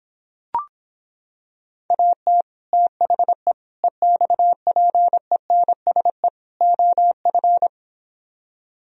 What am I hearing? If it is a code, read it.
Morse code, heard as a keyed tone, about 26 wpm, 700 Hz. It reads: AT THE EXPENSE OF